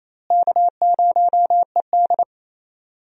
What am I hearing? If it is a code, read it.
Morse code, heard as a keyed tone, about 28 wpm, 700 Hz. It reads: K0ED
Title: USA Call Signs - 4 Characters - Repeated 3x - 28wpm